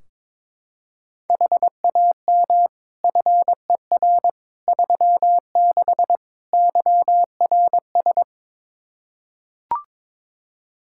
Morse code, heard as a keyed tone, about 22 wpm, 700 Hz. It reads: HAM FER 36 YRS
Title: K1USN SST QSO Practice 22wpm